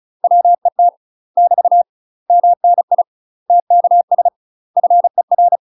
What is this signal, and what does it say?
Morse code, heard as a keyed tone, about 34 wpm, 700 Hz. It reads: WET <BT> MNI TKS FER
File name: UK MASTER QSOs TRAINING / 35WPM / UK MASTER QSOs 0003